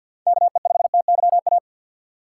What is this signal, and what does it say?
Morse code, heard as a keyed tone, about 46 wpm, 700 Hz. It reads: KE5TYA
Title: USA Call Signs - 6x Character Spacing - 50wpm